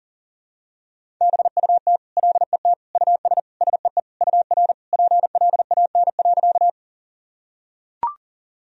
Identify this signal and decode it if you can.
Morse code, heard as a keyed tone, about 37 wpm, 700 Hz. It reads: BUT LET US SEE UR PLAN.
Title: Sentences from Top 500 Words and Abbreviations 40wpm